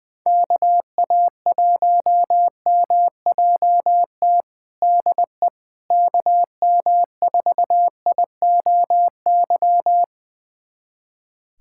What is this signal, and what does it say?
Morse code, heard as a keyed tone, about 20 wpm, 700 Hz. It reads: KA1MJT DE KM4IOY
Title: USA Callsign DE Callsign - 6 Characters - 20wpm